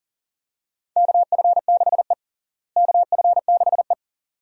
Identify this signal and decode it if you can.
Morse code, heard as a keyed tone, about 40 wpm, 700 Hz. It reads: KF6E KF6E
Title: USA Call Signs - 4 Characters - Repeated 2x - 40wpm